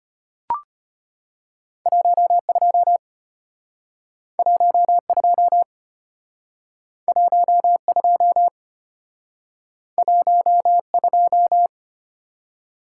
Morse code, heard as a keyed tone, about 32 wpm, 700 Hz. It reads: 12 12 12 12